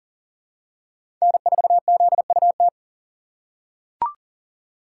Morse code, heard as a keyed tone, about 40 wpm, 700 Hz. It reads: N4ZUT
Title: USA Call Signs - 5 Characters - 3x Character Spacing - 40wpm